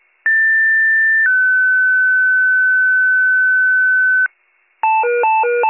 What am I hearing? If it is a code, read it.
Two-tone page, heard as a one-second tone, then a three-second tone, about 1800 then 1500 Hz, with noise. It is heard from 0.2 to 4.3 s.